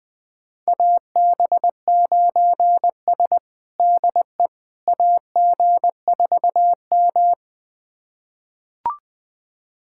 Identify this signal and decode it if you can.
Morse code, heard as a keyed tone, about 20 wpm, 700 Hz. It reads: AB9S DE AG4M
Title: USA Callsign DE Callsign - 4 Characters - 20wpm